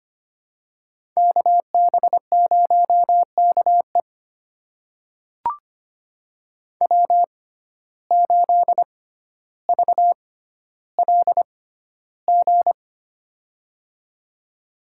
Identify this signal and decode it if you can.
Morse code, heard as a keyed tone, about 25 wpm, spaced as for 13 wpm, 700 Hz. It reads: KB0KE W8VLG
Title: USA Call Signs - 5 Characters - 6x Character Spacing - 25wpm